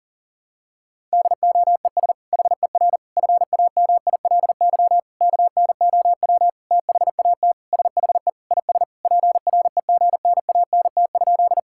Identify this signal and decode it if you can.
Morse code, heard as a keyed tone, about 40 wpm, 700 Hz. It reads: DOES HER FAMILY KNOW THAT SHE IS PREGNANT?
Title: Sentences from Top 1000 Words - Review Part 6 - 40wpm